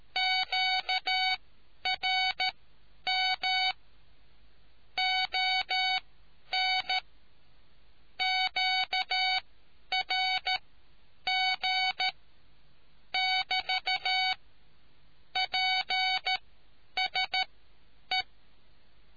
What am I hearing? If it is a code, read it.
Morse code, heard as a keyed tone, about 13 wpm, spaced as for 9 wpm, 750 Hz. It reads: QRM ON QRG <BT> PSE